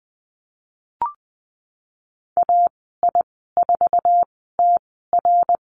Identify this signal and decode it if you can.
Morse code, heard as a keyed tone, about 20 wpm, spaced as for 15 wpm, 700 Hz. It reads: AI4TR
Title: 055 Lesson 10g - Letter r - Call Signs 20wpm